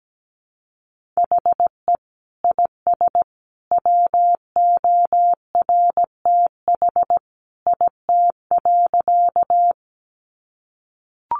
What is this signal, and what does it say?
Morse code, heard as a keyed tone, about 17 wpm, 700 Hz. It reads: HE IS WORTH IT.
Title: Sentences from Top 800 Words 17wpm